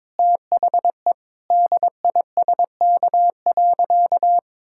Morse code, heard as a keyed tone, about 22 wpm, 700 Hz. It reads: THE DISK.
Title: Sentences from Top 1000 Words - Review Part 6 - 22wpm